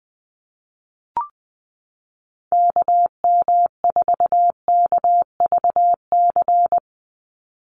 Morse code, heard as a keyed tone, about 20 wpm, 700 Hz. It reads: KM4KVC
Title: USA Call Signs 20wpm